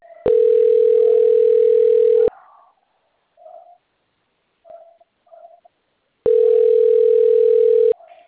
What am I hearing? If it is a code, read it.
Ringback tone (North America), 440 + 480 Hz.